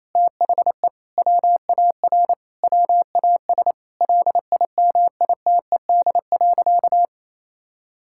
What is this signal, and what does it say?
Morse code, heard as a keyed tone, about 28 wpm, 700 Hz. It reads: THE WAR WAS LIMITED.
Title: Sentences from Top 1000 Words - Review Part 6 - 28wpm